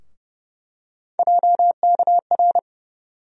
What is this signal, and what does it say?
Morse code, heard as a keyed tone, about 30 wpm, 700 Hz. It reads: JKR
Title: USA Call Sign Postfixes - 3 Letters 30wpm